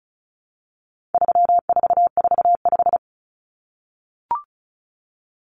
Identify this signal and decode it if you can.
Morse code, heard as a keyed tone, about 34 wpm, 700 Hz. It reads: 3445
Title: Numbers - 4-Digit 35wpm